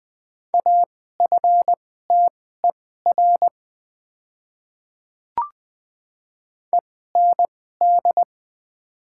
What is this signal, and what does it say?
Morse code, heard as a keyed tone, about 20 wpm, spaced as for 14 wpm, 700 Hz. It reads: AFTER END
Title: 2x Character Spacing - Top 300 Words - 20wpm